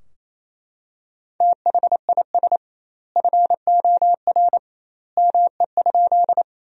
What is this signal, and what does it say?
Morse code, heard as a keyed tone, about 28 wpm, 700 Hz. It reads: THIS FOR ME?
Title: 2x Word-Spacing - Sentences from Top 200 Words - 28wpm